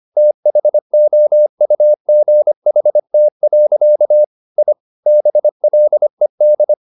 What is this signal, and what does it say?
Morse code, heard as a keyed tone, about 25 wpm, 600 Hz. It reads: THOUGHT. I BLED